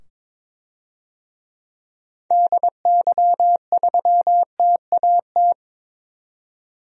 Morse code, heard as a keyed tone, about 21 wpm, 700 Hz. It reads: DY3TAT
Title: DXCC-100 Call Signs - 6 Characters - 22wpm